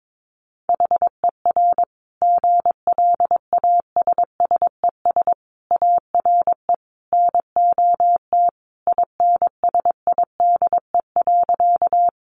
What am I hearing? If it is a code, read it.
Morse code, heard as a keyed tone, about 22 wpm, 700 Hz. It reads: HER GLASSES ARE NOT INSIDE.
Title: Additional Easy Phrases 22wpm